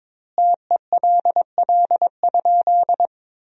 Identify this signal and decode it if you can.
Morse code, heard as a keyed tone, about 22 wpm, 700 Hz. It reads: TELL?
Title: 2x Word-Spacing - Sentences from Top 300 Words - 22wpm